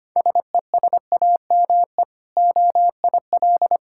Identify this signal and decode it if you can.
Morse code, heard as a keyed tone, about 25 wpm, 700 Hz. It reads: SESAME OIL
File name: practice / 25wpm / Difficult Words with Hints - 25wpm